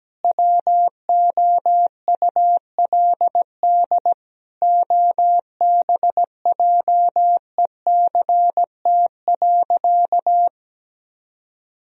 Morse code, heard as a keyed tone, about 17 wpm, 700 Hz. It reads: WOULD OBJECT.
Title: Sentences from Top 1000 Words - Review Part 3 - 17wpm